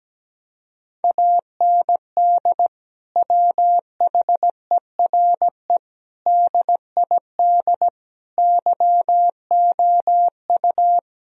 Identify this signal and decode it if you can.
Morse code, heard as a keyed tone, about 17 wpm, 700 Hz. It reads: AND WHERE DID YOU
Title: Sentences from Top 1000 Words - Review Part 1 - 17wpm